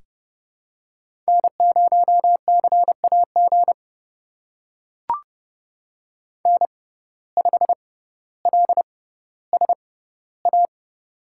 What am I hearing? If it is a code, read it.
Morse code, heard as a keyed tone, about 30 wpm, spaced as for 15 wpm, 700 Hz. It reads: N0CAG N5LSA
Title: USA Call Signs - 5 Characters - 6x Character Spacing - 30wpm